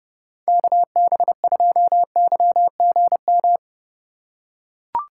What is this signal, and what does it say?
Morse code, heard as a keyed tone, about 30 wpm, 700 Hz. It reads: KB2YGM
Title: USA Call Signs - 6 Characters - 2x Character Spacing - 30wpm